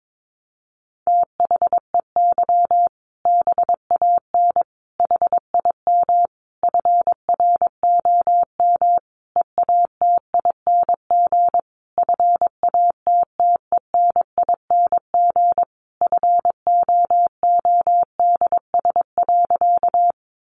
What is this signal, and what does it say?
Morse code, heard as a keyed tone, about 22 wpm, 700 Hz. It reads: THEY BAN HIM FROM EATING FATTENING FOODS.
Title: Easy Phrases 22wpm